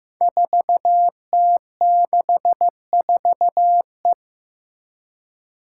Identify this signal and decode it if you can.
Morse code, heard as a keyed tone, about 15 wpm, 700 Hz. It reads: 4T64E
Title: Five Challenging Letter-Number Combinations 15wpm